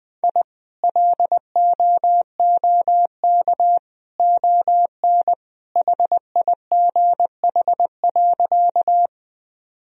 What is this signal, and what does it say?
Morse code, heard as a keyed tone, about 21 wpm, 700 Hz. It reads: I LOOK ON HIGH.
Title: Sentences from Top 1000 Words - Review Part 1 - 20wpm